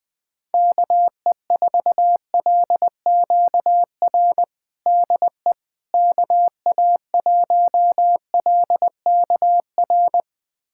Morse code, heard as a keyed tone, about 20 wpm, 700 Hz. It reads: KE4LQR DE KA1LKR
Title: USA Callsign DE Callsign - 6 Characters - 20wpm